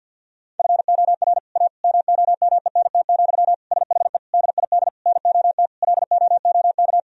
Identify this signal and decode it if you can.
Morse code, heard as a keyed tone, about 50 wpm, 700 Hz. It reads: FOR A MOMENT, SHE DID NOT LOOK
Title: Sentences from Top 1000 Words - Review Part 2 - 50wpm